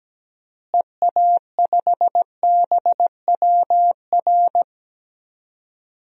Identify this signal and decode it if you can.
Morse code, heard as a keyed tone, about 17 wpm, 700 Hz. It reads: EA5BWR
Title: DXCC-100 Call Signs - 6 Characters - Repeated 3x - 17wpm